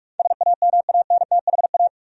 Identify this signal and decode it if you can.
Morse code, heard as a keyed tone, about 46 wpm, 700 Hz. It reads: SAMANTHA
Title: USA Common Names - 45wpm